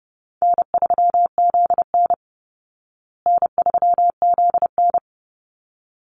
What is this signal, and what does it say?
Morse code, heard as a keyed tone, about 29 wpm, 700 Hz. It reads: N3ZN N3ZN
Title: Easy Call Signs Repeated 2x - 30wpm